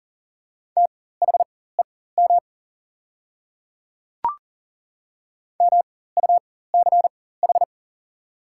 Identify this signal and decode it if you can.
Morse code, heard as a keyed tone, about 38 wpm, spaced as for 18 wpm, 700 Hz. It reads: THEM MUCH
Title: 4x Character Spacing - Top 200 Words - 40wpm